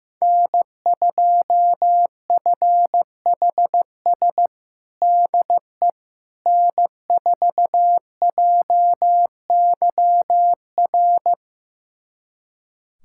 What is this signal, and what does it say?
Morse code, heard as a keyed tone, about 15 wpm, 700 Hz. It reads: N2FHS DE N4JYR
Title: USA Callsign DE Callsign - 5 Characters - 15wpm